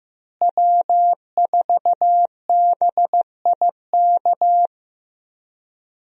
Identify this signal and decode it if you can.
Morse code, heard as a keyed tone, about 15 wpm, 700 Hz. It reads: W4BIK
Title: USA Call Signs - 5 Characters - Repeated 2x - 15wpm